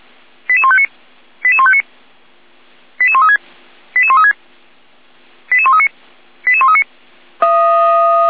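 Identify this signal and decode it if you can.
Five-tone selective call (ZVEI), heard as quick runs of five tones, tones of about 70 ms. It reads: 88159 88159 88136 88136 88139 88139
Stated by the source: ZVEI five-tone selective call 88397 received by SDR.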